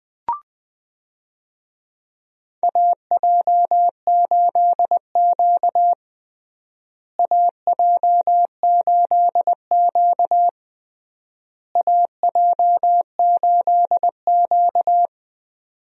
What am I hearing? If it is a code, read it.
Morse code, heard as a keyed tone, about 21 wpm, 700 Hz. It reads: AJ8Q AJ8Q AJ8Q
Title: USA Call Signs - 4 Characters - Repeated 3x - 20wpm